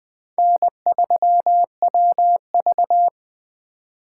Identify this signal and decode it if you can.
Morse code, heard as a keyed tone, about 20 wpm, 700 Hz. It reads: N3WV